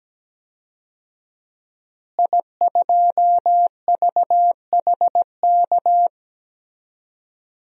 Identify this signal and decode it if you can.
Morse code, heard as a keyed tone, about 17 wpm, 700 Hz. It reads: I2VHK
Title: DXCC-100 Call Signs - 5 Characters - Repeated 3x - 17wpm